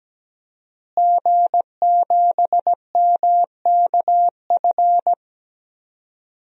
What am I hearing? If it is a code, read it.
Morse code, heard as a keyed tone, about 17 wpm, 700 Hz. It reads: G7MKF
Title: DXCC-100 Call Signs - 5 Characters - Repeated 2x - 17wpm